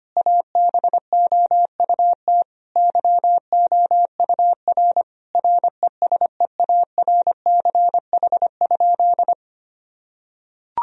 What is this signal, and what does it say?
Morse code, heard as a keyed tone, about 25 wpm, 700 Hz. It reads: ABOUT YOUR RESEARCH?
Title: Sentences from Top 1000 Words - Review Part 4 - 25wpm